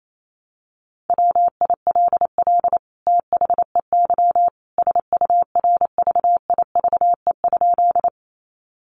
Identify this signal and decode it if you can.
Morse code, heard as a keyed tone, about 28 wpm, 700 Hz. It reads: WILL THEY SURVIVE?
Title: Sentences from Top 1300 Words 28wpm